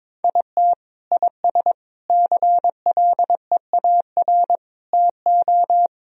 Morse code, heard as a keyed tone, about 22 wpm, 700 Hz. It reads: IT IS CLEAR TO
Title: Sentences from Top 1000 Words - Review Part 2 - 22wpm